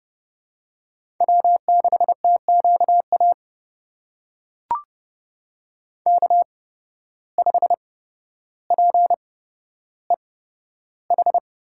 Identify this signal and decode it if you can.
Morse code, heard as a keyed tone, about 30 wpm, spaced as for 13 wpm, 700 Hz. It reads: W6TQA K5PEH
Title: USA Call Signs - 5 Characters - 8x Character Spacing - 30wpm